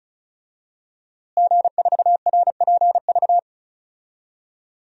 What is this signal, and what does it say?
Morse code, heard as a keyed tone, about 36 wpm, 700 Hz. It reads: G4RPV
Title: DXCC-100 Call Signs - 5 Characters - Repeated 3x - 35wpm